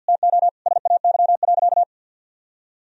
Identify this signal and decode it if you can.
Morse code, heard as a keyed tone, about 50 wpm, 700 Hz. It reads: TO SAY.
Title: Sentences from Top 1000 Words - Review Part 4 - 50wpm